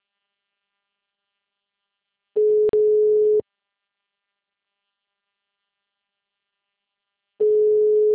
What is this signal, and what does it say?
Ringback tone (Europe, 425 Hz).